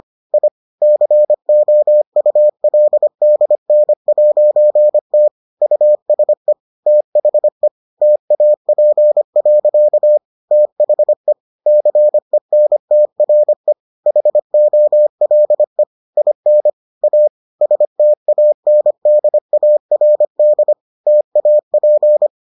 Morse code, heard as a keyed tone, about 25 wpm, 600 Hz. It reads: I COULDN'T USE THE TAP. THE CENTRE HOLE IN A STANDARD TAP